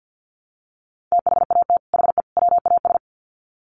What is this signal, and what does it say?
Morse code, heard as a keyed tone, about 50 wpm, 700 Hz. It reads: THAT HE WAS
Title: Sets of 3 Words 50wpm